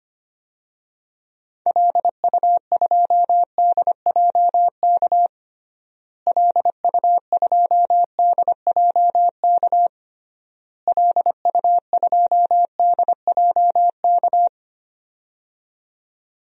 Morse code, heard as a keyed tone, about 25 wpm, 700 Hz. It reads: LU2DJK LU2DJK LU2DJK
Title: DXCC-100 Call Signs - 6 Characters - Repeated 3x - 25wpm